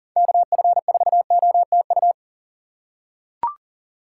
Morse code, heard as a keyed tone, about 40 wpm, 700 Hz. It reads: KF4OTU
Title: USA Call Signs - 6 Characters - 4x Character Spacing - 40wpm